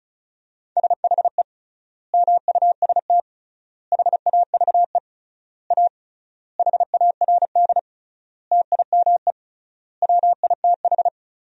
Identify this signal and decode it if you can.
Morse code, heard as a keyed tone, about 36 wpm, 700 Hz. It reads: SHE MUST HAVE A HARD TIME WITH
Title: 3x Word-Spacing - Sentences from Top 300 Words - 35wpm